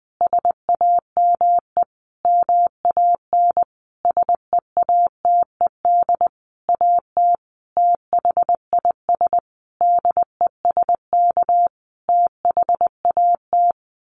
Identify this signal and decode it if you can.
Morse code, heard as a keyed tone, about 20 wpm, 700 Hz. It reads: SAME MAN SEATED AT THIS DESK THAT
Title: A Sound Of Thunder 20wpm